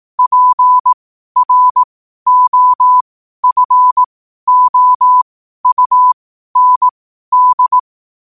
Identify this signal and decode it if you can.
Morse code, heard as a keyed tone, about 18 wpm, spaced as for 12 wpm, 1000 Hz. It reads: PROFOUND
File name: qotdfiles / 12wpm